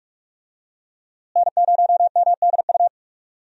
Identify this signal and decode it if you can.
Morse code, heard as a keyed tone, about 44 wpm, 700 Hz. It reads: N0MDU
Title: USA Call Signs - 5 Characters - Repeated 3x - 45wpm